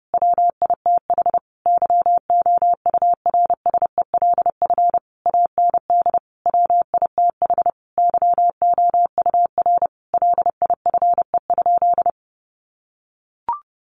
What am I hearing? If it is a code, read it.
Morse code, heard as a keyed tone, about 30 wpm, 700 Hz. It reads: WITH YOURSELF AND WITH YOUR LIFE?
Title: Sentences from Top 900 Words 30wpm